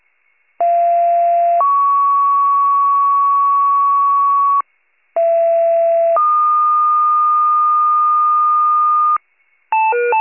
Two-tone page, heard as a one-second tone, then a three-second tone, about 690 then 1120 Hz, about 670 then 1220 Hz. It is heard from 0.6 to 9.2 s.